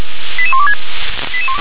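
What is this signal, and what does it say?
Five-tone selective call (ZVEI), heard as quick runs of five tones, tones of about 70 ms. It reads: 88126
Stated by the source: ZVEI five-tone selective call received on SDR.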